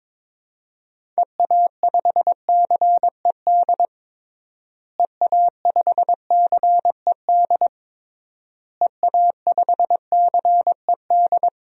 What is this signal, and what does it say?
Morse code, heard as a keyed tone, about 22 wpm, 700 Hz. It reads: EA5CED EA5CED EA5CED
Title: DXCC-100 Call Signs - 6 Characters - Repeated 3x - 22wpm